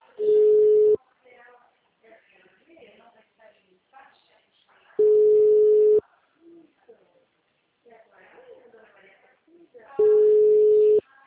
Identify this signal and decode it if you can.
Ringback tone (Europe, 425 Hz), with speech.